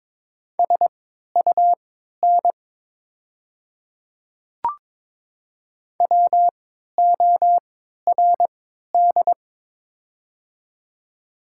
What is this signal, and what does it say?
Morse code, heard as a keyed tone, about 21 wpm, spaced as for 12 wpm, 700 Hz. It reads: SUN WORD
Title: 3x Character Spacing - Top 300 Words - 22wpm